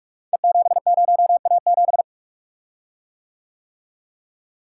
Morse code, heard as a keyed tone, about 44 wpm, 700 Hz. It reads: E70A7
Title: Code Groups - 5 Letter:Numbers - 45wpm